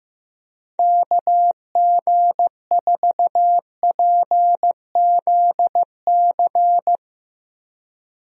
Morse code, heard as a keyed tone, about 16 wpm, 700 Hz. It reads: KG4PZC
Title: USA Call Signs - 6 Characters - Repeated 3x - 15wpm